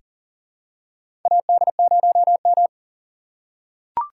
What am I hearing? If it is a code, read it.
Morse code, heard as a keyed tone, about 40 wpm, 700 Hz. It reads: AD0M
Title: USA Call Signs - 2x Character Spacing - 40wpm